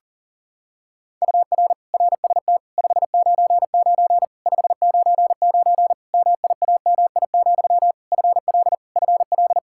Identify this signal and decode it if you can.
Morse code, heard as a keyed tone, about 40 wpm, 700 Hz. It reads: UR RST 599 599 MIAMI, FL FL
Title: Practice QSO - Exchange 1 - Beginning - 40wpm